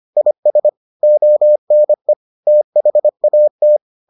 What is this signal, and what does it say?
Morse code, heard as a keyed tone, about 25 wpm, 600 Hz. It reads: IS ONE THAT